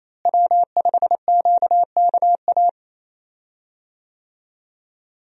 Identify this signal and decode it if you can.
Morse code, heard as a keyed tone, about 28 wpm, 700 Hz. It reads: W5QKA
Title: USA Call Signs - 5 Characters - Repeated 2x - 28wpm